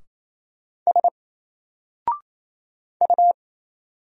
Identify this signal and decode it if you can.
Morse code, heard as a keyed tone, about 28 wpm, 700 Hz. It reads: S U
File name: Single-Letter-Number-28wpm